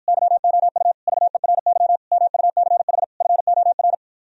Morse code, heard as a keyed tone, about 53 wpm, 700 Hz. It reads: YOU VERY MUCH FOR